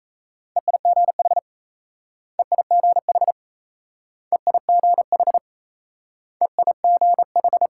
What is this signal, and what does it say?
Morse code, heard as a keyed tone, about 34 wpm, 700 Hz. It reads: EIGH EIGH EIGH EIGH
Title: Speed-Racing Multi-letter Phonemes - 28wpm